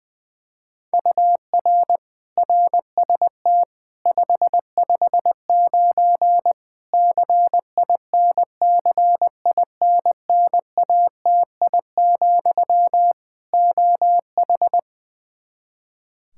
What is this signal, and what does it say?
Morse code, heard as a keyed tone, about 21 wpm, 700 Hz. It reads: UR RST 559 CINCINNATI, OH
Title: Practice QSO - Exchange 1 - Beginning - 20wpm